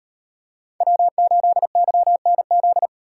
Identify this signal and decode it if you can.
Morse code, heard as a keyed tone, about 35 wpm, 700 Hz. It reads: W8YNZ